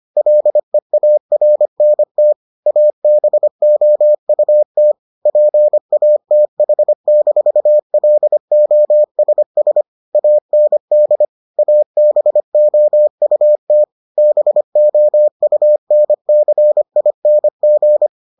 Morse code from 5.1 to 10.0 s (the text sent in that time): PATH-LOSS